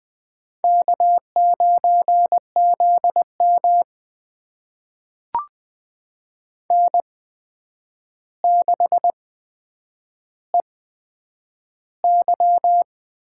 Morse code, heard as a keyed tone, about 20 wpm, spaced as for 9 wpm, 700 Hz. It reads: K9ZM N6EY
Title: USA Call Signs - 4 Characters - 8x Character Spacing - 20wpm